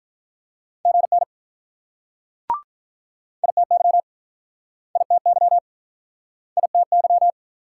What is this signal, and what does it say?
Morse code, heard as a keyed tone, about 48 wpm, 700 Hz. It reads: GN ITY ITY ITY